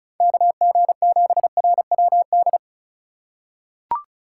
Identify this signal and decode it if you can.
Morse code, heard as a keyed tone, about 36 wpm, 700 Hz. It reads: KG7RWD
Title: USA Call Signs - 4x Character Spacing - 35wpm